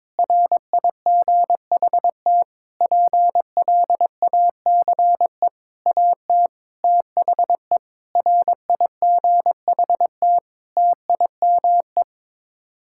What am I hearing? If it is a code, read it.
Morse code, heard as a keyed tone, about 22 wpm, 700 Hz. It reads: RIGHT PLACE AT THE RIGHT TIME
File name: Sets of 9 Words - 22wpm